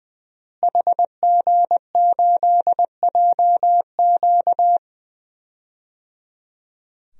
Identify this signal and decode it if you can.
Morse code, heard as a keyed tone, about 21 wpm, 700 Hz. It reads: HG8JQ
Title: DXCC-100 Call Signs - 5 Characters - Repeated 3x - 20wpm